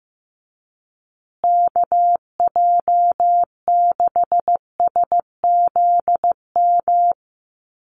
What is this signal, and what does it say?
Morse code, heard as a keyed tone, about 15 wpm, 700 Hz. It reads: KJ6SZM
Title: USA Call Signs Repeated 2x 15wpm